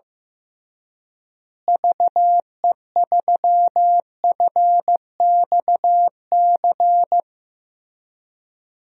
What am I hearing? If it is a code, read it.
Morse code, heard as a keyed tone, about 15 wpm, 700 Hz. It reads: VE3FXC
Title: DXCC-100 Call Signs - 6 Characters - Repeated 3x - 15wpm